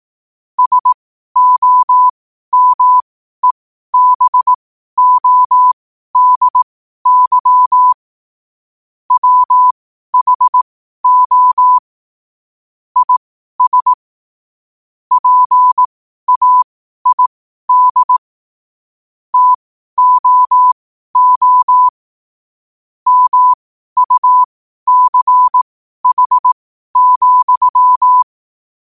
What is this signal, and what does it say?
Morse code, heard as a keyed tone, about 18 wpm, spaced as for 12 wpm, 1000 Hz. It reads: SOMEBODY WHO IS PAID TOO MUCH,
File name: qotdfiles / 12wpm